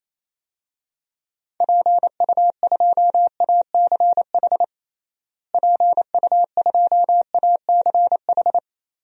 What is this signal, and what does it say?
Morse code, heard as a keyed tone, about 28 wpm, 700 Hz. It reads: PU2ACH PU2ACH
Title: DXCC-100 Call Signs - 6 Characters - Repeated 3x - 28wpm